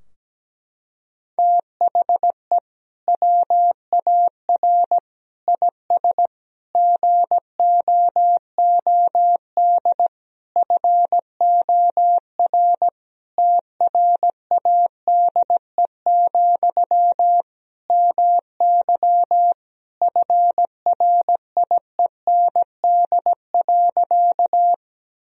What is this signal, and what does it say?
Morse code, heard as a keyed tone, about 17 wpm, 700 Hz. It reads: THE WAR IS GOOD FOR TRADE, MY FRIEND.
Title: Sentences from Top 1000 Words - Review Part 4 - 17wpm